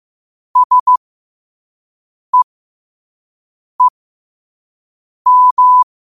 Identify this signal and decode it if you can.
Morse code, heard as a keyed tone, about 15 wpm, spaced as for 5 wpm, 1000 Hz. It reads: SEEM
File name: qotdfiles / 5wpm